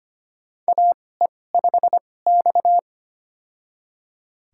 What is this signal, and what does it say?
Morse code, heard as a keyed tone, about 25 wpm, spaced as for 18 wpm, 700 Hz. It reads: AE5X
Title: USA Call Signs - 4 Characters - 2x Character Spacing - 25wpm